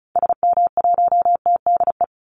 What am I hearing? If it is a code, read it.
Morse code, heard as a keyed tone, about 34 wpm, 700 Hz. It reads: SM1TDE
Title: Call Signs Repeated 3x - 35wpm